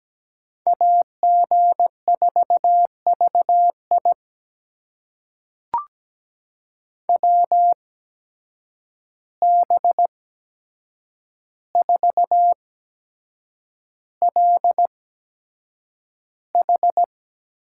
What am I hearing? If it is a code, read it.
Morse code, heard as a keyed tone, about 17 wpm, spaced as for 7 wpm, 700 Hz. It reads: AG4VI WB4LH
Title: USA Call Signs - 8x Character Spacing - 17wpm